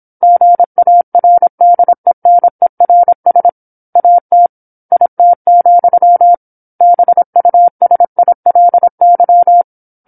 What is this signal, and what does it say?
Morse code, heard as a keyed tone, about 27 wpm, 700 Hz. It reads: GARDENERS AT IT, BUSILY
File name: 08-Chapter-08 26 WPM